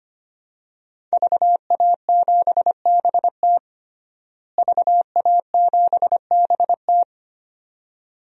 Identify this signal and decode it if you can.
Morse code, heard as a keyed tone, about 25 wpm, 700 Hz. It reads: VA7BT VA7BT
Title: DXCC-100 Call Signs - 5 Characters - Repeated 2x - 25wpm